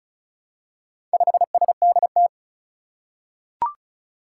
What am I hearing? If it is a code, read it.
Morse code, heard as a keyed tone, about 34 wpm, 700 Hz. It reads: 5SDT
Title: Four Challenging Letter-Number Combinations 35wpm